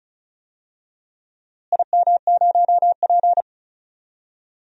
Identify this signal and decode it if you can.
Morse code, heard as a keyed tone, about 34 wpm, 700 Hz. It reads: IM0P
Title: DXCC-100 Call Signs - 4 Characters - Repeated 3x - 35wpm